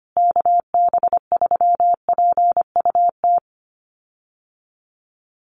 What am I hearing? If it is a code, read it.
Morse code, heard as a keyed tone, about 25 wpm, 700 Hz. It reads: KB3PUT
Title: USA Call Signs Repeated 3x 25wpm